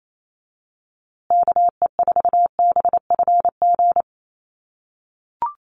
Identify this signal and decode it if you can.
Morse code, heard as a keyed tone, about 28 wpm, 700 Hz. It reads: KE4BFG
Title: USA Call Signs Repeated 3x 28wpm